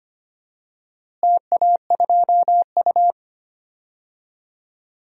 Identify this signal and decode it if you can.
Morse code, heard as a keyed tone, about 25 wpm, 700 Hz. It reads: TA2U
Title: DXCC-100 Call Signs - 4 Characters - Repeated 3x - 25wpm